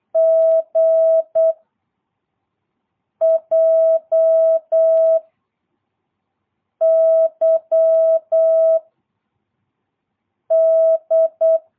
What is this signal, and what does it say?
Morse code, heard as a keyed tone, about 8 wpm, 660 Hz. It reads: G J Y D
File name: jh8cbh / 2022cw